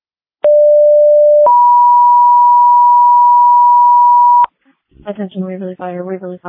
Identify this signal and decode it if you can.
Two-tone page, heard as a one-second tone, then a three-second tone, about 590 then 970 Hz, with speech. It is heard from 0.4 to 4.4 s.